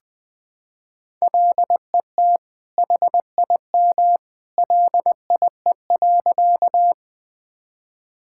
Morse code, heard as a keyed tone, about 20 wpm, 700 Hz. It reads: LET HIM LIE.
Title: Sentences from Top 1600 Words 20wpm